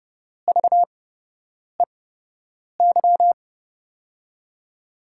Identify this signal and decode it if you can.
Morse code, heard as a keyed tone, about 30 wpm, spaced as for 8 wpm, 700 Hz. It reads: VEY
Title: USA Call Signs - 6 Characters - 8x Character Spacing - 30wpm